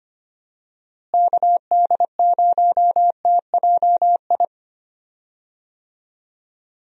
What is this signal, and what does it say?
Morse code, heard as a keyed tone, about 25 wpm, 700 Hz. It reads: KD0TJI
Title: USA Call Signs - 6 Characters - Repeated 3x - 25wpm